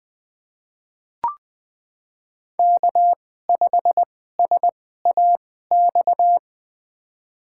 Morse code, heard as a keyed tone, about 20 wpm, spaced as for 14 wpm, 700 Hz. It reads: K5SAX
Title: USA Call Signs - 5 Characters - 2x Character Spacing - 20wpm